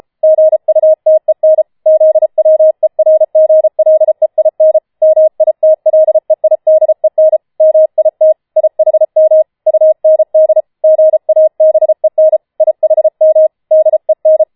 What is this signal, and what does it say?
Morse code, heard as a keyed tone, about 33 wpm, 630 Hz. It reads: GUTEN ZWERGLEIN MITLEIDEN MIT IHM UND GABEN IHM DEN
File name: Schneewittchen Teil5 34wpm